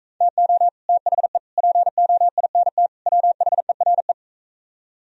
Morse code, heard as a keyed tone, about 45 wpm, 700 Hz. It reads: TO THE POINT WHERE